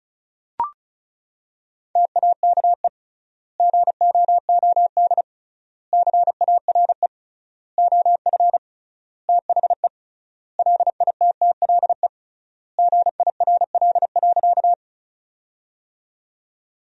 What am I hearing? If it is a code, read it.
Morse code, heard as a keyed tone, about 36 wpm, 700 Hz. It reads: TAKE GOOD CARE OF THE LITTLE GIRL.
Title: 3x Word-Spacing - Sentences from Top 400 Words - 35wpm